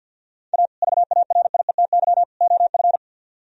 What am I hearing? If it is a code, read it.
Morse code, heard as a keyed tone, about 64 wpm, 700 Hz. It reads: A VARIETY OF